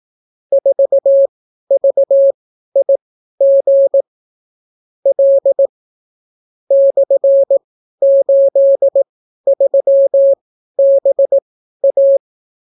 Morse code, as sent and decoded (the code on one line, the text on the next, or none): ....- ...- .. --. / .-.. / -..-. ---.. ...-- -... .-
4VIG L /83BA